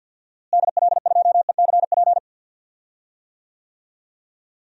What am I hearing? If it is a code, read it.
Morse code, heard as a keyed tone, about 52 wpm, 700 Hz. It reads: DL2ECP
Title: DXCC-100 Call Signs - 6 Characters - Repeated 3x - 50wpm